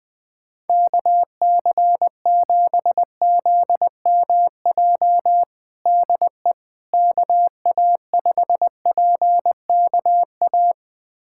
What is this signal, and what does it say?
Morse code, heard as a keyed tone, about 21 wpm, 700 Hz. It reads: KC7ZMJ DE KA5PKA
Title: USA Callsign DE Callsign - 6 Characters - 20wpm